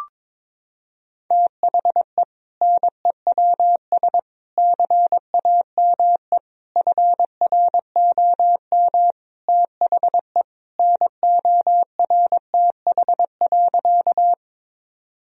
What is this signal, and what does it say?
Morse code, heard as a keyed tone, about 22 wpm, 700 Hz. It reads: THE NEWS CAME FROM THE NORTH.
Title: Sentences from Top 1000 Words - Review Part 1 - 22wpm